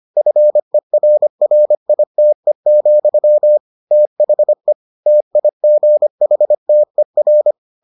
Morse code, heard as a keyed tone, about 25 wpm, 600 Hz. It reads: FERRITE, THE TIGHTER